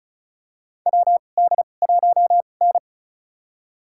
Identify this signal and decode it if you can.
Morse code, heard as a keyed tone, about 34 wpm, spaced as for 25 wpm, 700 Hz. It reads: WD1N
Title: USA Call Signs - 2x Character Spacing - 35wpm